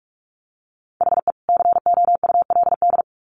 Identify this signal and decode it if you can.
Morse code, heard as a keyed tone, about 44 wpm, 700 Hz. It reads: HE COULD
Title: Sets of 2 Words 45wpm